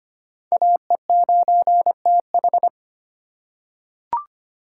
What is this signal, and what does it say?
Morse code, heard as a keyed tone, about 25 wpm, 700 Hz. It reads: AE9TH
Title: USA Call Signs - 2x Character Spacing - 25wpm